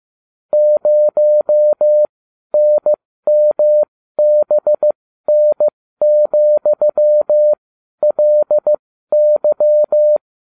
Morse code, heard as a keyed tone, about 15 wpm, spaced as for 12 wpm, 600 Hz. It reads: 0NMBN,LY